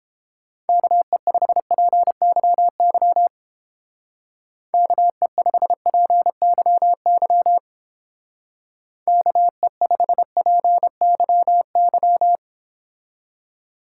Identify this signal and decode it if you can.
Morse code, heard as a keyed tone, about 30 wpm, 700 Hz. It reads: KE5PYY KE5PYY KE5PYY